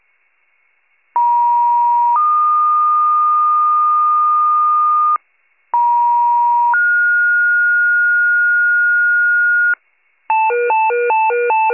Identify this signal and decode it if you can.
Two-tone page, heard as a one-second tone, then a three-second tone, about 950 then 1220 Hz, about 950 then 1430 Hz, with noise. It is heard from 1.2 to 9.7 s.